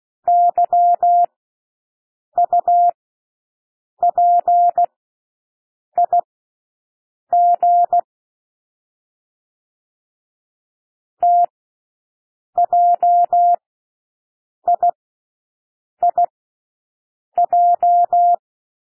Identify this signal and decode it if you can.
Morse code, heard as a keyed tone, about 16 wpm, spaced as for 6 wpm, 700 Hz. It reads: YUPIG TJIIJ
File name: cwmp3 / L1 SOUND LETTERS 16 wpm